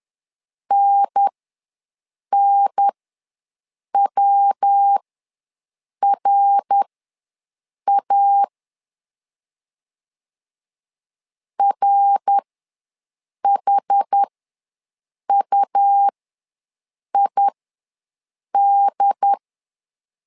Morse code, heard as a keyed tone, about 11 wpm, spaced as for 6 wpm, 800 Hz. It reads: NNWRA RHUID
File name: LEZIONI / 3 MWRO / 3 MWR ws 10wpm 2151